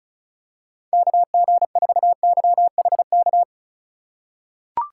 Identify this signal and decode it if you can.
Morse code, heard as a keyed tone, about 36 wpm, 700 Hz. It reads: KG4YHK
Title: USA Call Signs - 6 Characters - 8x Character Spacing - 35wpm